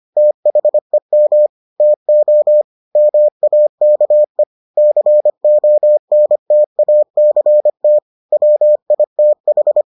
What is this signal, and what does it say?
Morse code, heard as a keyed tone, about 25 wpm, 600 Hz. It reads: THEM TO MAKE CONTACT WITH